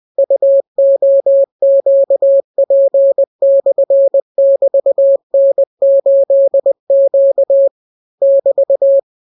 Morse code, heard as a keyed tone, about 20 wpm, 550 Hz. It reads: UOQP/<BT>N8Q <BT>